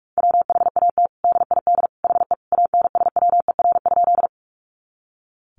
Morse code, heard as a keyed tone, about 44 wpm, 700 Hz. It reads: WHAT DID HE ANSWER?
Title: Sentences from Top 300 Words 45wpm